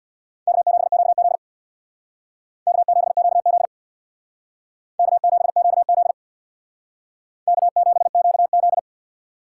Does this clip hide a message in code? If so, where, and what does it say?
Yes: Morse code from 0.5 to 8.8 s, heard as a keyed tone, about 63 wpm, 700 Hz. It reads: K7QZ K7QZ K7QZ K7QZ